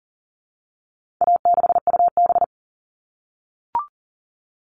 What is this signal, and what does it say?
Morse code, heard as a keyed tone, about 40 wpm, 700 Hz. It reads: A6UB
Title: All Cut Numbers - 4-Digit 40wpm